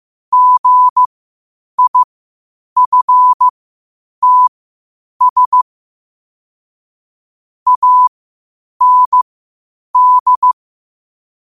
Morse code, heard as a keyed tone, about 15 wpm, spaced as for 8 wpm, 1000 Hz. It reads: GIFTS AND
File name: qotdfiles / 8wpm